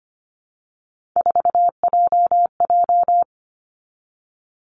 Morse code, heard as a keyed tone, about 25 wpm, 700 Hz. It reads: 4JJ
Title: Three Challenging Letter-Number Combinations 25wpm